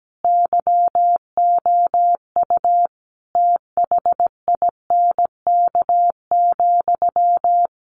Morse code, heard as a keyed tone, about 17 wpm, 700 Hz. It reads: YOU THINK,
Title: A Sound Of Thunder 17wpm